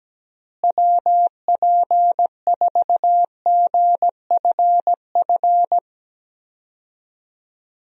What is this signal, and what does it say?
Morse code, heard as a keyed tone, about 17 wpm, 700 Hz. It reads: WP4GFF
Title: USA Call Signs - 6 Characters - Repeated 3x - 17wpm